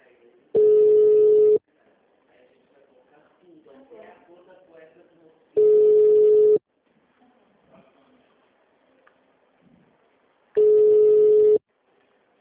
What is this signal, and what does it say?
Ringback tone (Europe, 425 Hz), with speech.
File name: out- -2007- - - .2703